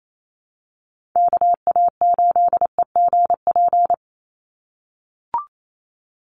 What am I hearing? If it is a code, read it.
Morse code, heard as a keyed tone, about 28 wpm, 700 Hz. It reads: KA8EGP
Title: USA Call Signs 28wpm